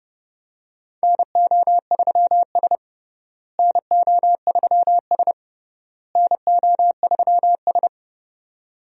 Morse code, heard as a keyed tone, about 30 wpm, 700 Hz. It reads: NO3S NO3S NO3S
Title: USA Call Signs - 4 Characters - Repeated 3x - 30wpm